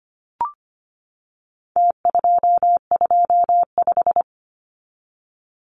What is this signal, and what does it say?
Morse code, heard as a keyed tone, about 25 wpm, 700 Hz. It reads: T225
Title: Numbers - Contest Style - 4-Digit 25wpm